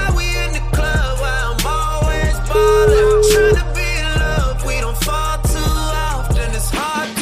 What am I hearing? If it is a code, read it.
Ringback tone (Europe, 425 Hz), with noise.
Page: Uvítací melodii uslyší volající, než přijmete hovor.